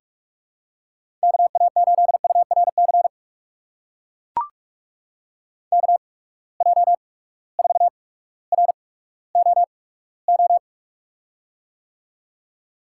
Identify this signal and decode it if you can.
Morse code, heard as a keyed tone, about 44 wpm, spaced as for 19 wpm, 700 Hz. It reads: KA8URC KJ4ROO
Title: USA Call Signs - 6 Characters - 8x Character Spacing - 45wpm